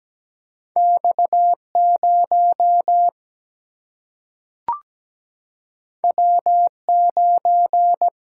Morse code, heard as a keyed tone, about 17 wpm, 700 Hz. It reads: X 0 W 9
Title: Single-Letter Single-Number 17wpm